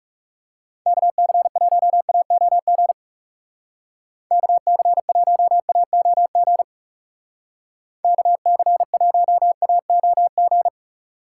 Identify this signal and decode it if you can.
Morse code, heard as a keyed tone, about 40 wpm, 700 Hz. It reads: KC1AOG KC1AOG KC1AOG